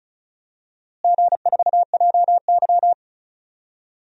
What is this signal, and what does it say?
Morse code, heard as a keyed tone, about 34 wpm, 700 Hz. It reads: G4JY
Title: DXCC-100 Call Signs - 4 Characters - Repeated 3x - 35wpm